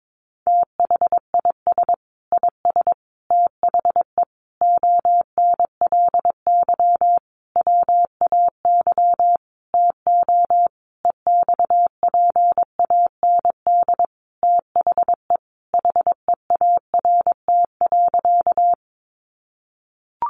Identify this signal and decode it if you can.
Morse code, heard as a keyed tone, about 22 wpm, 700 Hz. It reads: THIS IS THE ONLY WAY TO EXPAND THE HEART.
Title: Sentences from Top 1400 Words 22wpm